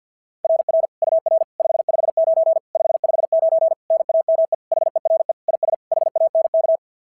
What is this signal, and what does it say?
Morse code, heard as a keyed tone, about 51 wpm, 690 Hz. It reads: RR UR 559 559 NAME HERE IS HANK
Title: SOTA QSO Practice 50wpm